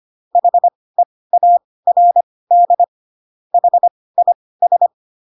Morse code, heard as a keyed tone, about 25 wpm, spaced as for 18 wpm, 700 Hz. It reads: HEARD HIS